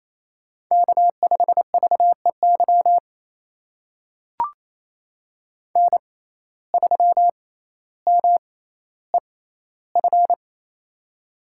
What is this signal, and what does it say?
Morse code, heard as a keyed tone, about 27 wpm, spaced as for 14 wpm, 700 Hz. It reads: K5VEY N3MEF
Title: USA Call Signs - 5 Characters - 6x Character Spacing - 28wpm